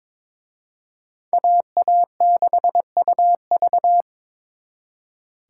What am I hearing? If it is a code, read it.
Morse code, heard as a keyed tone, about 26 wpm, 700 Hz. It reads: AA6UV
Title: Speed Racing - USA 2x2 Call Signs 22wpm